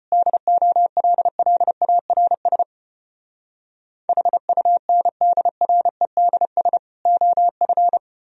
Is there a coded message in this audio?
Yes: Morse code from 0.1 to 8.0 s, heard as a keyed tone, about 33 wpm, 700 Hz. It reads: DOLLARS HUNDREDS OF